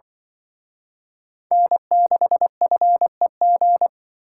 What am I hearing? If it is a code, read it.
Morse code, heard as a keyed tone, about 24 wpm, 700 Hz. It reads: N6FEG
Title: Speed Racing - USA 1x3 Call Signs 20wpm